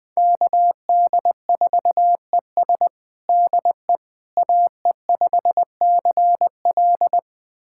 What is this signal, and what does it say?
Morse code, heard as a keyed tone, about 20 wpm, 700 Hz. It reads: KD4ES DE AE5CL
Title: USA Callsign DE Callsign - 5 Characters - 20wpm